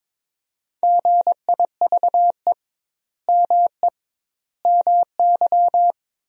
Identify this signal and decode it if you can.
Morse code, heard as a keyed tone, about 21 wpm, 700 Hz. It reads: GIVE ME MY
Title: 2x Word-Spacing - Sentences from Top 200 Words - 22wpm